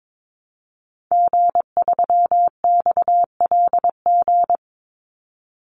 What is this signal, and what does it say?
Morse code, heard as a keyed tone, about 22 wpm, 700 Hz. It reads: G3XLG
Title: Call Signs - 22wpm